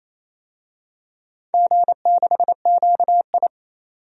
Morse code, heard as a keyed tone, about 28 wpm, 700 Hz. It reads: G6QI
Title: DXCC-100 Call Signs - 4 Characters - 28wpm